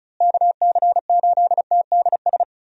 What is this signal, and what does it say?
Morse code, heard as a keyed tone, about 34 wpm, 700 Hz. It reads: KC8TDS
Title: USA Call Signs - 6 Characters - Repeated 3x - 35wpm